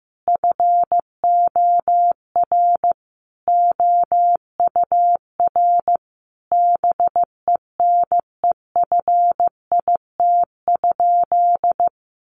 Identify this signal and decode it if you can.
Morse code, heard as a keyed tone, about 15 wpm, 700 Hz. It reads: FOR OUR BENEFIT?
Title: Sentences from Top 1500 Words 15wpm